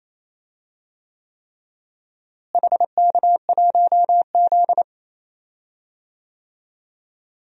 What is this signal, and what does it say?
Morse code, heard as a keyed tone, about 28 wpm, 700 Hz. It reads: HK1Z
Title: DXCC-100 Call Signs - 4 Characters - Repeated 3x - 28wpm